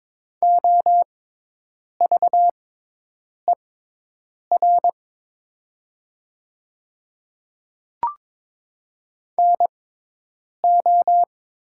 Morse code, heard as a keyed tone, about 22 wpm, spaced as for 8 wpm, 700 Hz. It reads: OVER NO
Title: 6x Character Spacing - Top 200 Words - 22wpm